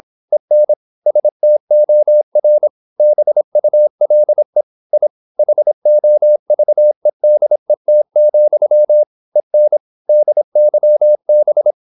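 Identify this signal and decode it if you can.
Morse code, heard as a keyed tone, about 26 wpm, 600 Hz. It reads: EN STOR BULE I HOVEDET, EN DYB
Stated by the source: High-speed (QRQ) Morse at 26 wpm.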